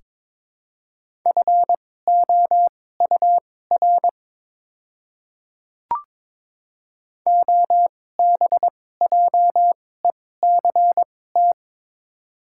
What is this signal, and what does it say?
Morse code, heard as a keyed tone, about 22 wpm, spaced as for 16 wpm, 700 Hz. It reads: FOUR OBJECT
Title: 2x Character Spacing - Top 500 Words - 22wpm